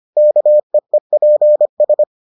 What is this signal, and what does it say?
Morse code, heard as a keyed tone, about 25 wpm, 600 Hz. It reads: KEEPS